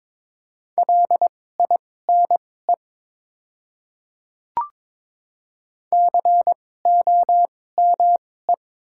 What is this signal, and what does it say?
Morse code, heard as a keyed tone, about 22 wpm, spaced as for 16 wpm, 700 Hz. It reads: LINE COME
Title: 2x Character Spacing - Top 500 Words - 22wpm